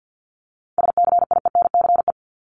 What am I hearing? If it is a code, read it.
Morse code, heard as a keyed tone, about 50 wpm, 700 Hz. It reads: SCIENCE
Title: Top 500 Words 50wpm